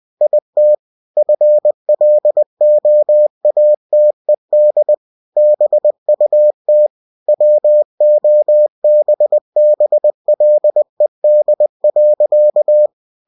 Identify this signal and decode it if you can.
Morse code, heard as a keyed tone, about 20 wpm, 600 Hz. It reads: IT FLOATED BUT WOBBLED.
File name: cw-books / winnie-the-pooh / winnie-20wpm-page-0100